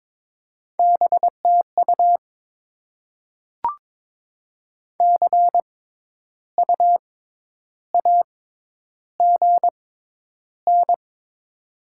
Morse code, heard as a keyed tone, about 21 wpm, spaced as for 10 wpm, 700 Hz. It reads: BTU CU AGN
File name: Top 20 QSO Elements - 6x Character Spacing - 22wpm